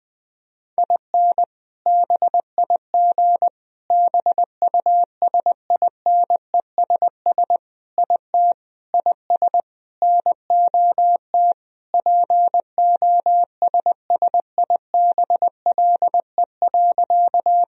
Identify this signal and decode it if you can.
Morse code, heard as a keyed tone, about 20 wpm, 700 Hz. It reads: IN BIG BUSINESS IT IS NOT POSSIBLE.
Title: Sentences from Top 1000 Words - Review Part 1 - 20wpm